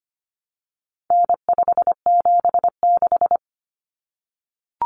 Morse code, heard as a keyed tone, about 25 wpm, 700 Hz. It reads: N576
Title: Common Cut Numbers - 4-Digit 25wpm